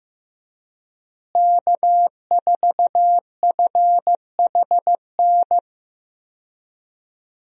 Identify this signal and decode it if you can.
Morse code, heard as a keyed tone, about 15 wpm, 700 Hz. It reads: K4FHN
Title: USA Call Signs - 5 Characters - Repeated 2x - 15wpm